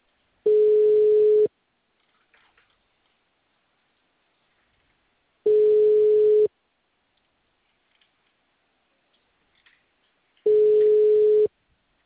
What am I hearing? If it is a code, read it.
Ringback tone (Europe, 425 Hz).
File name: out- -2010- - - .510